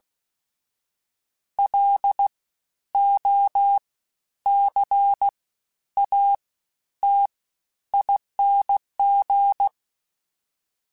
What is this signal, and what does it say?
Morse code, heard as a keyed tone, about 16 wpm, spaced as for 10 wpm, 800 Hz. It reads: LOCATING
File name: Special INGs-mp3s / Special INGs-mp3s / Fast ing @15